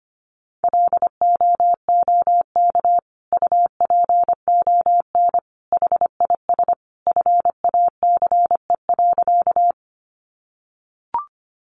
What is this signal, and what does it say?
Morse code, heard as a keyed tone, about 25 wpm, 700 Hz. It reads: LOOK UPON HIS FACE.
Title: Sentences from Top 1100 Words 25wpm